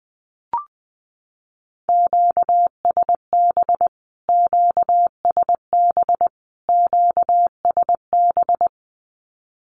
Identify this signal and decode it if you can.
Morse code, heard as a keyed tone, about 20 wpm, 700 Hz. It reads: QSB QSB QSB
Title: Q-Codes Repeated 3x 20wpm